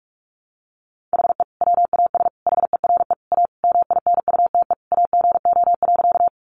Morse code, heard as a keyed tone, about 44 wpm, 700 Hz. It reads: HE WAS HERE A MINUTE AGO.
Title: Sentences from Top 500 Words 45wpm